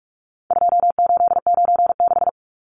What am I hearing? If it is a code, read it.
Morse code, heard as a keyed tone, about 44 wpm, 700 Hz. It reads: 2896
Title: Numbers - 4-Digit 45wpm